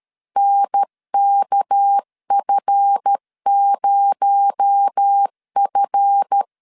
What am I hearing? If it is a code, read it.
Morse code, heard as a keyed tone, about 13 wpm, 800 Hz. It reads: NKF0F
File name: LEZIONI / 7 GF9Z / 7 GF 12wpm 2122